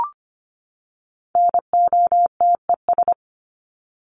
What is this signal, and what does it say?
Morse code, heard as a keyed tone, about 25 wpm, 700 Hz. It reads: NOTES
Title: ES:S Suffix - Top 100 Words with Base Words 25wpm